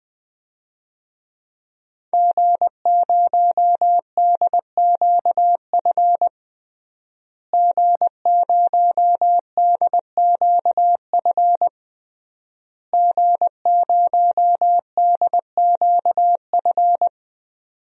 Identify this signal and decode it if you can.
Morse code, heard as a keyed tone, about 20 wpm, 700 Hz. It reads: G0DQF G0DQF G0DQF
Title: DXCC-100 Call Signs - 5 Characters - Repeated 3x - 20wpm